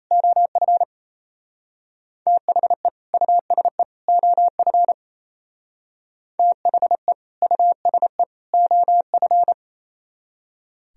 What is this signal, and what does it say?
Morse code, heard as a keyed tone, about 34 wpm, 700 Hz. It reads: OF THE USE OF THE USE OF
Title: Speed-Racing - Sets of 3 Words 28wpm